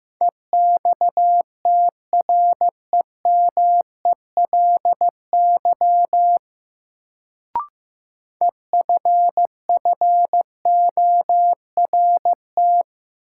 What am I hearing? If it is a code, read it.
Morse code, heard as a keyed tone, about 15 wpm, 700 Hz. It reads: EXTREMELY EFFORT
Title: Sentences from Top 1200 Words - New Word Drill Alt - 15wpm